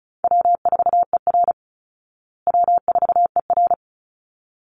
Morse code, heard as a keyed tone, about 34 wpm, 700 Hz. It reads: W4ER W4ER
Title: Call Signs Repeated 3x - 35wpm